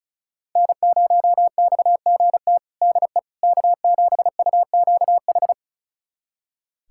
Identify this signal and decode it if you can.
Morse code, heard as a keyed tone, about 36 wpm, 700 Hz. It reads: N0XGT DE K7UQH
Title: USA Callsign DE Callsign - 5 Characters - 35wpm